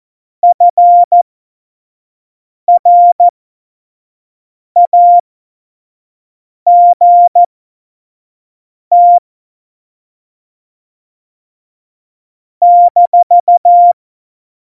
Morse code, heard as a keyed tone, about 14 wpm, spaced as for 5 wpm, 700 Hz. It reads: FRAGT -